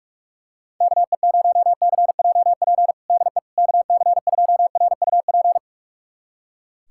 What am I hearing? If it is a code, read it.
Morse code, heard as a keyed tone, about 41 wpm, 690 Hz. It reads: KE0CJP DE KC2RUP
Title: USA Callsign DE Callsign - 6 Characters - 45wpm